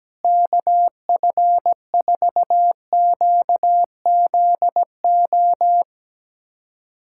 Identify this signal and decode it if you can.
Morse code, heard as a keyed tone, about 17 wpm, 700 Hz. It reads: KF4QZO
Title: USA Callsign DE Callsign - 6 Characters - 17wpm